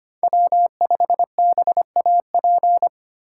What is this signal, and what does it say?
Morse code, heard as a keyed tone, about 25 wpm, 700 Hz. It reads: W5BAP
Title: USA Call Signs - 5 Characters - 4x Character Spacing - 25wpm